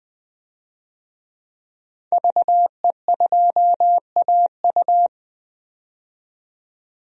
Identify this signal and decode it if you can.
Morse code, heard as a keyed tone, about 20 wpm, 700 Hz. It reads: VE2AU
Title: DXCC-100 Call Signs - 5 Characters - 20wpm